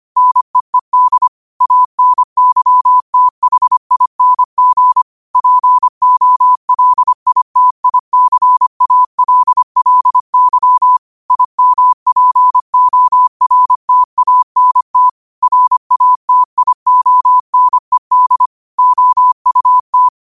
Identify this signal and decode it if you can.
Morse code, heard as a keyed tone, about 25 wpm, 1000 Hz. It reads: NEED ANYTHING POLITICALLY IMPORTANT RATIONED OUT
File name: qotdfiles / 25wpm